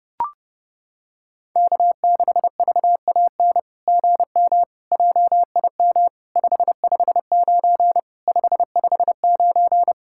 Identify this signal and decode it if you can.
Morse code, heard as a keyed tone, about 30 wpm, 700 Hz. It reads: K6VAN GM JIM 559 559
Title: SOTA QSO Practice 30wpm